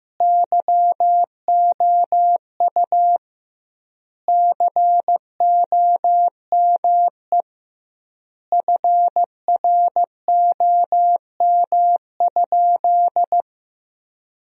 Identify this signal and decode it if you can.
Morse code, heard as a keyed tone, about 15 wpm, 700 Hz. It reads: YOU COME FROM?
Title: 2x Word-Spacing - Sentences from Top 200 Words - 15wpm